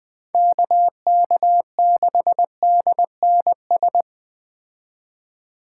Morse code, heard as a keyed tone, about 21 wpm, 700 Hz. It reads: KK6DNS